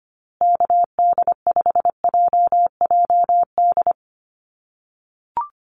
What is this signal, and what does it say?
Morse code, heard as a keyed tone, about 25 wpm, 700 Hz. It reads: KD5JJD
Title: USA Call Signs Repeated 3x 25wpm